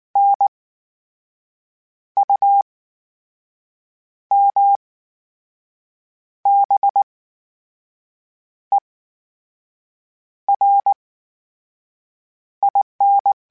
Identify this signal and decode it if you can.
Morse code, heard as a keyed tone, about 19 wpm, spaced as for 5 wpm, 800 Hz. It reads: NUMBERIN
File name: Special INGs / Special INGs-mp3s / VerySlowToFaster Ing @18